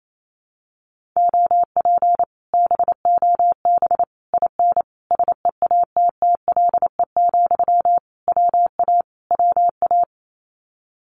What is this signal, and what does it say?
Morse code, heard as a keyed tone, about 28 wpm, 700 Hz. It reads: OP BOB IN SEATTLE, WA WA
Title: Common QSO Phrases Encore 28wpm